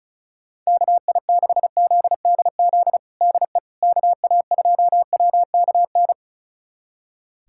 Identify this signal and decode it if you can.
Morse code, heard as a keyed tone, about 36 wpm, 690 Hz. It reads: KI6ZDZ DE KA2WKN
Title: USA Callsign DE Callsign 35wpm